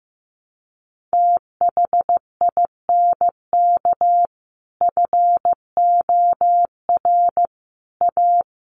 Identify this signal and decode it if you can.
Morse code, heard as a keyed tone, about 15 wpm, 700 Hz. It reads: THINK FOR A
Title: Sentences from Top 1000 Words 15wpm